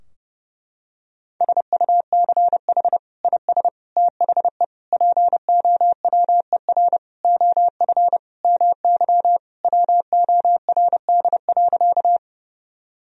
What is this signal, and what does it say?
Morse code, heard as a keyed tone, about 30 wpm, 700 Hz. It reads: SUCH IS THE POWER OF MY WORD.
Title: Sentences from Top 1000 Words - Review Part 2 - 30wpm